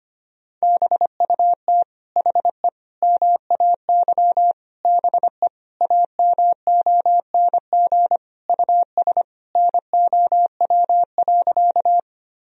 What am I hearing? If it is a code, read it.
Morse code, heard as a keyed tone, about 25 wpm, 700 Hz. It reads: BUT HE MAY BE AMONG US NOW.
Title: Sentences from Top 1000 Words - Review Part 3 - 25wpm